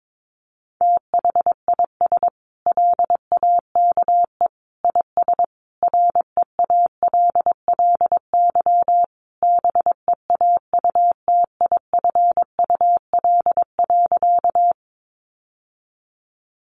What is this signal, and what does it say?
Morse code, heard as a keyed tone, about 22 wpm, 700 Hz. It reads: THIS LAKE IS REALLY BEAUTIFUL.
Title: Sentences from Top 1100 Words 22wpm